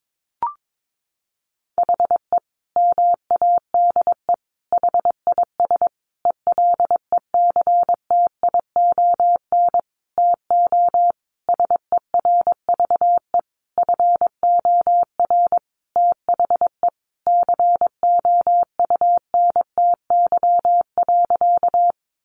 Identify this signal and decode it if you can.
Morse code, heard as a keyed tone, about 22 wpm, 700 Hz. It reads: HE MADE HIS ELECTION TO SERVE FOR THE COUNTY.
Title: Sentences from Top 1500 Words 22wpm